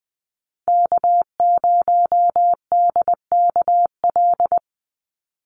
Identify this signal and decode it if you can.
Morse code, heard as a keyed tone, about 20 wpm, 700 Hz. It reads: K0DKL
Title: USA Call Signs 20wpm